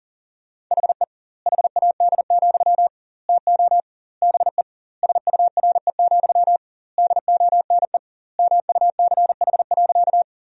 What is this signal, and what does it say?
Morse code, heard as a keyed tone, about 40 wpm, 700 Hz. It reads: HE HAD, TO BE SURE, DONE MUCH.
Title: 2x Word-Spacing - Sentences from Top 500 Words - 40wpm